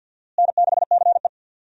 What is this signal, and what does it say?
Morse code, heard as a keyed tone, about 46 wpm, 700 Hz. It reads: N6CE
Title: USA Call Signs - 4 Characters - Repeated 3x - 50wpm